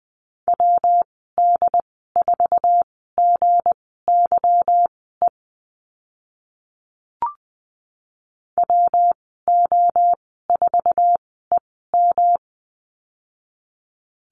Morse code, heard as a keyed tone, about 20 wpm, spaced as for 14 wpm, 700 Hz. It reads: WD4GYE WO4EM
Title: 173 Lesson 28g - Number 9 - Call Signs 20wpm